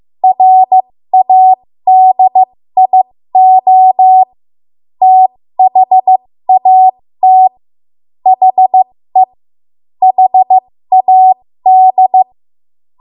Morse code, heard as a keyed tone, about 15 wpm, spaced as for 13 wpm, 750 Hz. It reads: RADIO THAT HE HAD
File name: Morse / 13WPM